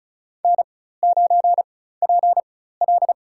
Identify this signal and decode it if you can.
Morse code, heard as a keyed tone, about 36 wpm, spaced as for 16 wpm, 700 Hz. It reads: N9PL
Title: USA Call Signs - 4 Characters - 4x Character Spacing - 35wpm